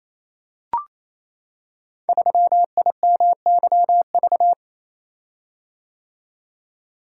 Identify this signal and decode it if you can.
Morse code, heard as a keyed tone, about 29 wpm, 700 Hz. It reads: 3IMYV
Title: Code Groups - 5 Letter:Numbers - 28wpm